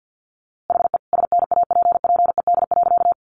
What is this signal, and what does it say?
Morse code, heard as a keyed tone, about 50 wpm, 700 Hz. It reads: HE SNAPPED.
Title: A Sound Of Thunder 50wpm